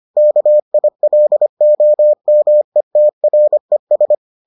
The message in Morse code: KILOMETRES

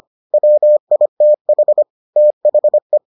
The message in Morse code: WITH THE

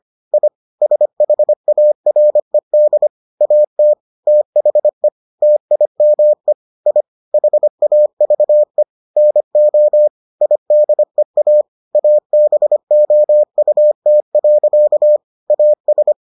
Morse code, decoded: I SHARED AT THE TIME I HAVE NO IDEA ABOUT. AS

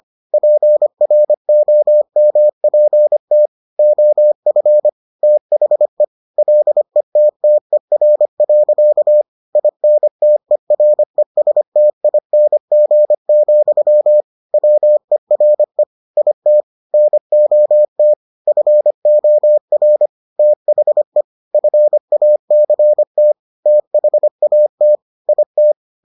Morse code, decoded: PROMPT OF THE LETTER. INTERESTING, WERE IT NOT FOR THE FACT THAT IT

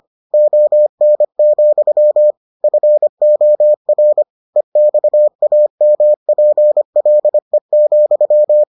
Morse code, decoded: ON, FOR EXAMPLE,

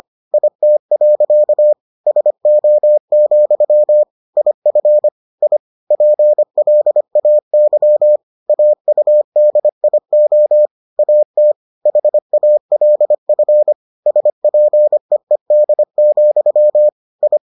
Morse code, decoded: IT. SO, IF I PLAY AUDIO AT HALF SPEED, I

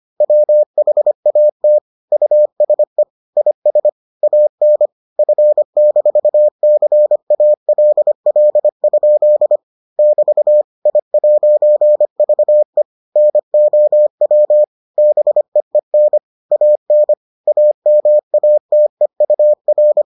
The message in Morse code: WHAT USE IS AN F-CALL? <BT> I'VE NOW BEEN AN AMATEUR

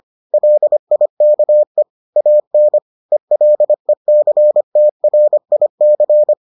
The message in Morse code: LIKE AN ELECTRIC

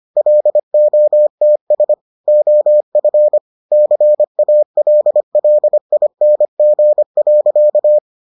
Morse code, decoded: LOTS OF CALLING.

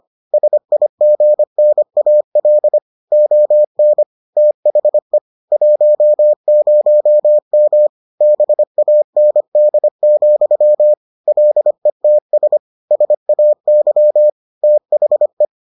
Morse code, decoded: SIGNAL ON THE 10M BAND, LETS SAY THE

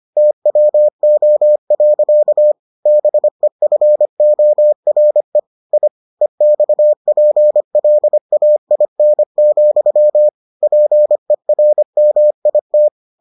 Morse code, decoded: TWO. BEFORE I EXPLAIN, PERMIT

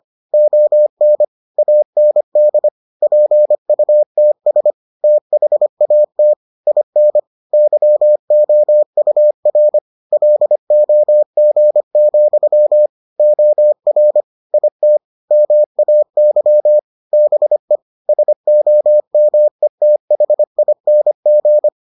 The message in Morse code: ON AND PUTS THAT IN YOUR LOG, OR IT MAY BE SOMETHING

